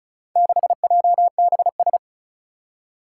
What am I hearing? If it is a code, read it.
Morse code, heard as a keyed tone, about 36 wpm, 700 Hz. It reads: 6JBS
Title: Four Challenging Letter-Number Combinations 35wpm